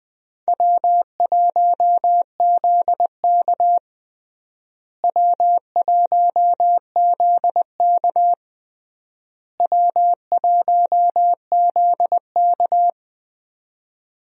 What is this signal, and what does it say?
Morse code, heard as a keyed tone, about 20 wpm, 700 Hz. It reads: W1ZK W1ZK W1ZK
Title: USA Call Signs - 4 Characters - Repeated 3x - 20wpm